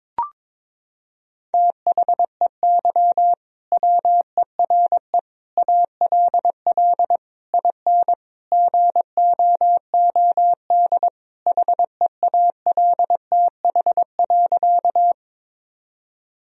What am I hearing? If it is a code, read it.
Morse code, heard as a keyed tone, about 22 wpm, 700 Hz. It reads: THEY WERE ALL IN GOOD HEALTH.
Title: Sentences from Top 1000 Words - Review Part 3 - 22wpm